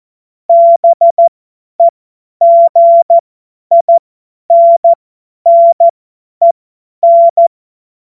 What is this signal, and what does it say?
Morse code, heard as a keyed tone, about 14 wpm, spaced as for 10 wpm, 680 Hz. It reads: BEGINNEN
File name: Morsekurs / Witze / Witz 22. Juni 2021, 50 F70